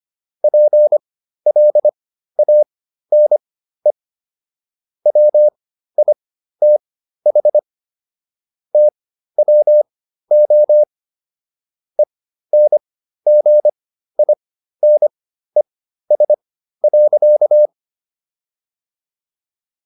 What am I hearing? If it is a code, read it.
Morse code, heard as a keyed tone, about 25 wpm, spaced as for 13 wpm, 600 Hz. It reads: PLANE WITH TWO ENGINES.